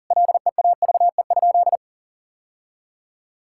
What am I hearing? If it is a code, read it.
Morse code, heard as a keyed tone, about 40 wpm, 700 Hz. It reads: LEAVE?
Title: Sentences from Top 1000 Words - Review Part 5 - 40wpm